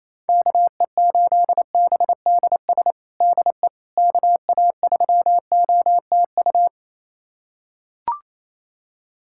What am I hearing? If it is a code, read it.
Morse code, heard as a keyed tone, about 28 wpm, 700 Hz. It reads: KE8BDS DE KA3OTU
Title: USA Callsign DE Callsign 28wpm